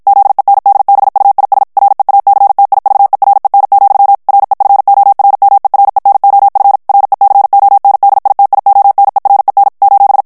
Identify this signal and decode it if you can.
Morse code, heard as a keyed tone, about 52 wpm, 800 Hz. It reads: ZENDBUIS REACTIVEREN, REFORMEREN OF RECONDITIONEREN :
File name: morse mp3 / HOLLANDAIS texte clair 2 vitesse 50 mots minute